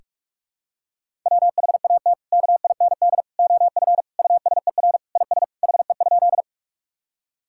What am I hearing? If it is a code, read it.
Morse code, heard as a keyed tone, about 41 wpm, 700 Hz. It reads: WHAT KIND OF USER IS HE?
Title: Sentences from Top 1000 Words - Review Part 5 - 45wpm